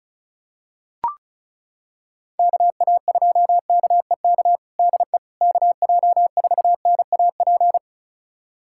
Morse code, heard as a keyed tone, about 36 wpm, 700 Hz. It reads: KA2KEK DE KJ4NAP
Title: USA Callsign DE Callsign - 6 Characters - 35wpm